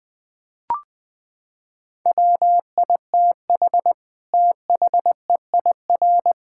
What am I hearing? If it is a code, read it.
Morse code, heard as a keyed tone, about 19 wpm, 700 Hz. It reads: WITH THEIR
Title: Sentences from Top 1000 Words - Review Part 6 - 20wpm